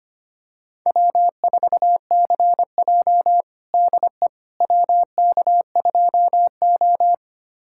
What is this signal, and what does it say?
Morse code, heard as a keyed tone, about 25 wpm, 700 Hz. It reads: W4CJ DE WK2O
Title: USA Callsign DE Callsign - 4 Characters - 25wpm